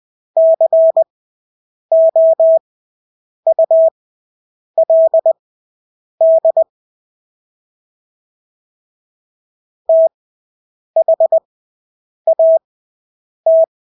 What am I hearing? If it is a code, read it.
Morse code, heard as a keyed tone, about 20 wpm, spaced as for 8 wpm, 650 Hz. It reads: COULD THAT
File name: Engelske ord 20/8 wpm